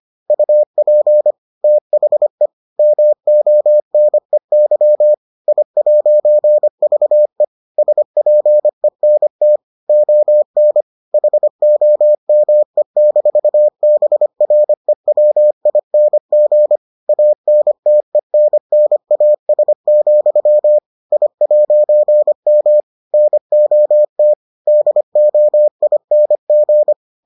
Morse code, heard as a keyed tone, about 25 wpm, 600 Hz. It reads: UP THE MONEY I'VE SPENT ON HOME-BREWING ANTENNAS, I'M NOT DOING